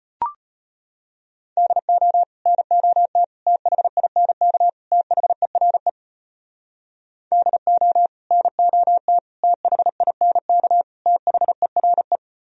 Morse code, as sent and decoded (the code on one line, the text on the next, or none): -.. --- / -. --- - / - .... .. -. -.- / - .... .-. / -.. --- / -. --- - / - .... .. -. -.- / - .... .-. .
DO NOT THINK THERE DO NOT THINK THERE